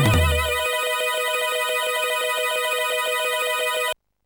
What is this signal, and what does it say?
There is no code.